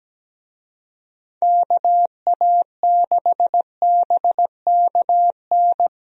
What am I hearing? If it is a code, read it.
Morse code, heard as a keyed tone, about 17 wpm, 700 Hz. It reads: KA6BKN
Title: USA Call Signs - 6 Characters - 17wpm